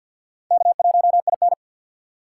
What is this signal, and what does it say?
Morse code, heard as a keyed tone, about 64 wpm, 700 Hz. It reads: K1IN